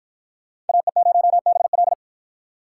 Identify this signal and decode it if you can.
Morse code, heard as a keyed tone, about 53 wpm, 700 Hz. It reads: AE0BL